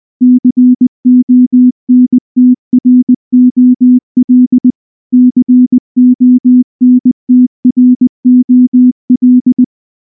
Morse code, heard as a keyed tone, about 19 wpm, 250 Hz. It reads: CONTROL CONTROL
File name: morse / morse-training-20WPM-1